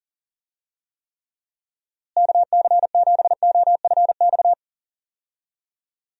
Morse code, heard as a keyed tone, about 40 wpm, 700 Hz. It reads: KC7OFX
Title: USA Call Signs - 6 Characters - Repeated 3x - 40wpm